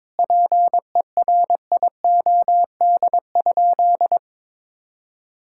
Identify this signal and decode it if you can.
Morse code, heard as a keyed tone, about 22 wpm, 700 Hz. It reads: PERIOD?
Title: Sentences from Top 1000 Words - Review Part 5 - 22wpm